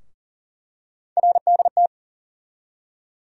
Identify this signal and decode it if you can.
Morse code, heard as a keyed tone, about 40 wpm, 700 Hz. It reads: RDT
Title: USA Call Sign Postfixes - 3 Letters 40wpm